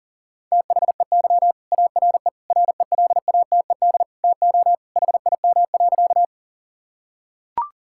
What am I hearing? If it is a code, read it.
Morse code, heard as a keyed tone, about 40 wpm, 700 Hz. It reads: THEY ARE RELATED TO HIM.
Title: Sentences from Top 1000 Words - Review Part 5 - 40wpm